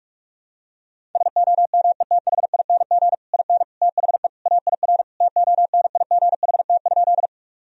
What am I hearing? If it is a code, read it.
Morse code, heard as a keyed tone, about 46 wpm, 700 Hz. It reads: SOMETHING IN THE AIR TONIGHT?
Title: Sentences from Top 1000 Words - Review Part 1 - 45wpm